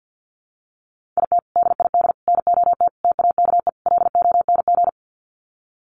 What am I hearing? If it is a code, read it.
Morse code, heard as a keyed tone, about 50 wpm, 700 Hz. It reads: IT DID NOT TAKE LONG
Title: Sets of 5 Words Encore2 50wpm